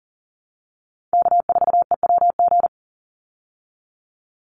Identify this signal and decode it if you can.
Morse code, heard as a keyed tone, about 40 wpm, 700 Hz. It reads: K4EWG
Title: Call Signs Repeated 2x - 40wpm